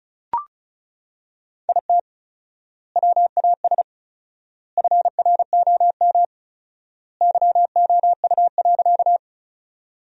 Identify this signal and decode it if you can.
Morse code, heard as a keyed tone, about 34 wpm, 700 Hz. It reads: IT WAS FROM YOU.
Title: 4x Word-Spacing - Sentences from Top 100 Words - 35wpm